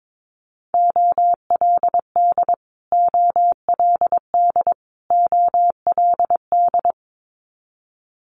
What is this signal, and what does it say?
Morse code, heard as a keyed tone, about 22 wpm, 700 Hz. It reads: OLD OLD OLD
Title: Multi-letter Phonemes Repeated 3x 22wpm